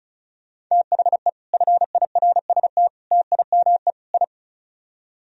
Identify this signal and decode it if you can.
Morse code, heard as a keyed tone, about 44 wpm, 700 Hz. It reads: THE FIRST TIME I